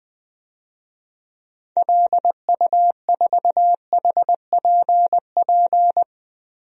Morse code, heard as a keyed tone, about 20 wpm, 700 Hz. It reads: LU4HPP
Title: DXCC-100 Call Signs - 6 Characters - 20wpm